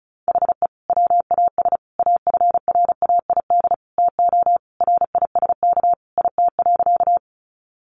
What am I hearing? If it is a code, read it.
Morse code, heard as a keyed tone, about 34 wpm, 700 Hz. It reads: HE WAS AFRAID TO RISK IT.
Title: Sentences from Top 1500 Words 35wpm